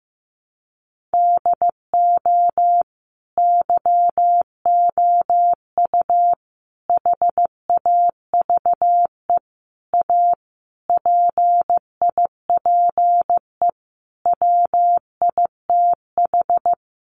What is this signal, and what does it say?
Morse code, heard as a keyed tone, about 15 wpm, 700 Hz. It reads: DO YOU HAVE A PIPE WITH
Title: Sentences from Top 1500 Words 15wpm